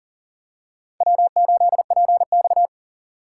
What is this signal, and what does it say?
Morse code, heard as a keyed tone, about 40 wpm, 700 Hz. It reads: W8PX
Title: USA Call Signs - 4 Characters - 6x Character Spacing - 40wpm